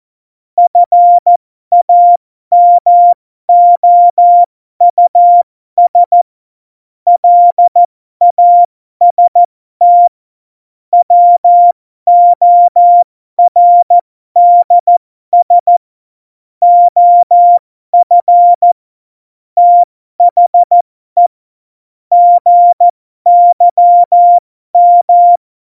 Morse code, heard as a keyed tone, about 14 wpm, spaced as for 12 wpm, 700 Hz. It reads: FAMOUS LAST WORDS OF THE GYM
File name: Morsekurs / Witze